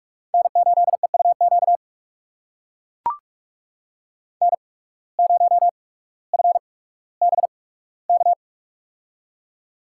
Morse code, heard as a keyed tone, about 44 wpm, spaced as for 19 wpm, 700 Hz. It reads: N8EUQ N0FBK
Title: USA Call Signs - 5 Characters - 8x Character Spacing - 45wpm